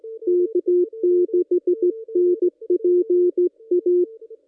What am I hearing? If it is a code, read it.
Morse code, heard as a keyed tone, about 17 wpm, 360 Hz. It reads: K6NPA
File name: contests / ss-cw-k5tr-op-n5rz -1600